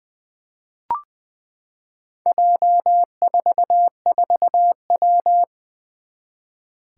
Morse code, heard as a keyed tone, about 20 wpm, 700 Hz. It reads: J44W
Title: Four Challenging Letter-Number Combinations 20wpm